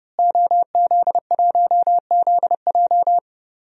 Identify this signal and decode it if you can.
Morse code, heard as a keyed tone, about 30 wpm, 700 Hz. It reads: OZ1ZJ
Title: DXCC-100 Call Signs - 5 Characters - Repeated 3x - 30wpm